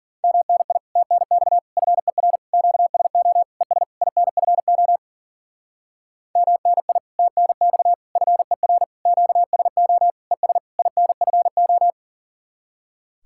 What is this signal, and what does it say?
Morse code, heard as a keyed tone, about 47 wpm, 700 Hz. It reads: MNI TNX FER QSO ES INFO MNI TNX FER QSO ES INFO
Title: Speed-Racing Common QSO Phrases Encore - 40wpm